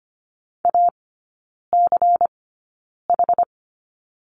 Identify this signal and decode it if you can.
Morse code, heard as a keyed tone, about 25 wpm, spaced as for 9 wpm, 700 Hz. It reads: ACH
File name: course / 25:10wpm / 093 Lesson 16e - Letter c - Three Characters 25wpm